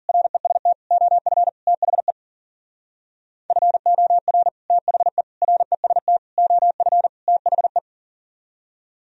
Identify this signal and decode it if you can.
Morse code, heard as a keyed tone, about 54 wpm, 700 Hz. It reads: REST OF THE FOR THE REST OF THE